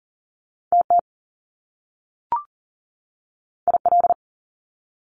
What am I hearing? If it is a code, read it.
Morse code, heard as a keyed tone, about 40 wpm, 700 Hz. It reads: TT IL
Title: Top Two Letter Combinations in Words 40wpm